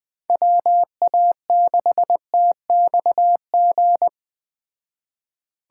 Morse code, heard as a keyed tone, about 20 wpm, 700 Hz. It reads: WA6TXG
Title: Speed Racing - USA 2x3 Call Signs 17wpm